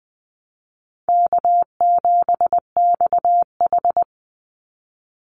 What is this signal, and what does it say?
Morse code, heard as a keyed tone, about 20 wpm, 700 Hz. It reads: K7XH
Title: Call Signs Repeated 3x - 20wpm